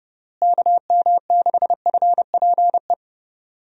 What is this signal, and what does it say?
Morse code, heard as a keyed tone, about 30 wpm, 700 Hz. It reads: KM6FPE
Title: USA Call Signs - 6 Characters - Repeated 2x - 30wpm